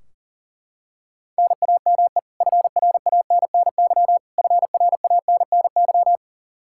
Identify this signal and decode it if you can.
Morse code, heard as a keyed tone, about 40 wpm, 700 Hz. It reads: NAME FRANNY FRANNY
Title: Practice QSO - Exchange 1 - Beginning - 40wpm